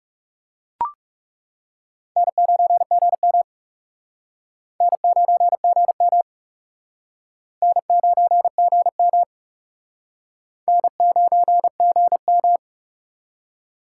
Morse code, heard as a keyed tone, about 38 wpm, 700 Hz. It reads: N9GM N9GM N9GM N9GM